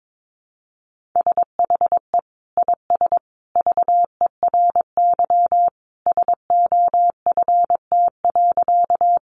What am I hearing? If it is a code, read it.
Morse code, heard as a keyed tone, about 22 wpm, 700 Hz. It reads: SHE IS VERY SOFT.
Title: Easy Phrases 22wpm